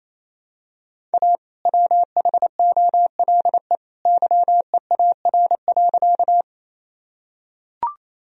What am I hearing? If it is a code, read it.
Morse code, heard as a keyed tone, about 28 wpm, 700 Hz. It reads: A WHOLE YEAR.
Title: Sentences from Top 1000 Words - Review Part 1 - 28wpm